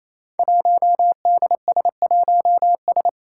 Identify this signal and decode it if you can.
Morse code, heard as a keyed tone, about 28 wpm, 700 Hz. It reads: 1DS1S
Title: Five Challenging Letter-Number Combinations 28wpm